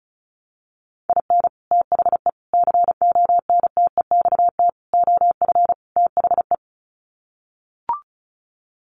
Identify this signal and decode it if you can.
Morse code, heard as a keyed tone, about 34 wpm, 700 Hz. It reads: IN THE CONTEXT OF THE
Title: Sets of 5 Words 35wpm